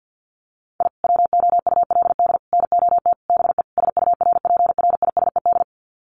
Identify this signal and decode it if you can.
Morse code, heard as a keyed tone, about 48 wpm, 700 Hz. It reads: I WOULD NOT BE SURPRISED
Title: Sets of 5 Words Encore2 50wpm